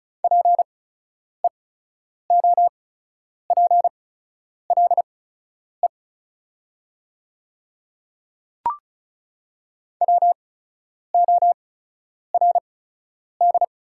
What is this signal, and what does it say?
Morse code, heard as a keyed tone, about 36 wpm, spaced as for 10 wpm, 700 Hz. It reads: PEOPLE WORD
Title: 8x Character Spacing - Top 100 Words - 35wpm